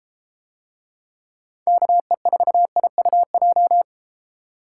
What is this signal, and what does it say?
Morse code, heard as a keyed tone, about 33 wpm, 700 Hz. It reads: KE4IUJ